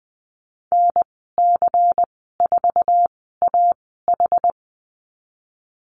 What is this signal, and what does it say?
Morse code, heard as a keyed tone, about 20 wpm, spaced as for 15 wpm, 700 Hz. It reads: NC4AH
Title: 186 Lesson 30g - Letter b - Call Signs 20wpm